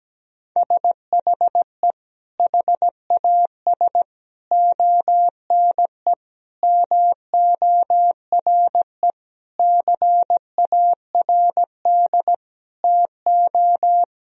Morse code, heard as a keyed tone, about 17 wpm, 700 Hz. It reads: SHE HAS ONE MORE CARD TO
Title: Sentences from Top 1000 Words - Review Part 4 - 17wpm